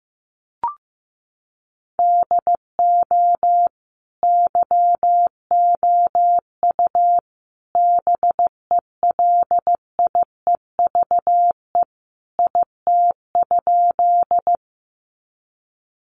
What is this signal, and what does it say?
Morse code, heard as a keyed tone, about 15 wpm, 700 Hz. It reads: DO YOU BELIEVE IT?
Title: Sentences from Top 1000 Words 15wpm